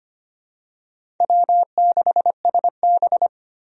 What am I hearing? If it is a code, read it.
Morse code, heard as a keyed tone, about 25 wpm, 700 Hz. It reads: W6SB
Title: USA Call Signs - 4 Characters - 25wpm